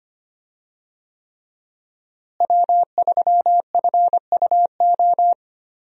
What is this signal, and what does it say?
Morse code, heard as a keyed tone, about 25 wpm, 700 Hz. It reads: W3FUO
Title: USA Call Signs - 5 Characters - Repeated 3x - 25wpm